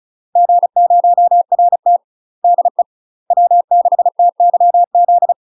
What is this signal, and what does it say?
Morse code, heard as a keyed tone, about 34 wpm, 700 Hz. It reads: G0RT DE W6TYZ
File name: UK MASTER QSOs TRAINING / 35WPM / UK MASTER QSOs 0040